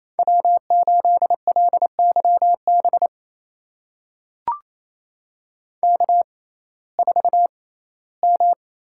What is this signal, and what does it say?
Morse code, heard as a keyed tone, about 28 wpm, spaced as for 17 wpm, 700 Hz. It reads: W8LYB K4M
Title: USA Call Signs - 5 Characters - 6x Character Spacing - 28wpm